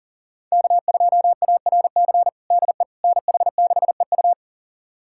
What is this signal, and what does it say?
Morse code, heard as a keyed tone, about 40 wpm, 700 Hz. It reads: K2ARC DE NH6EU
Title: USA Callsign DE Callsign - 5 Characters - 40wpm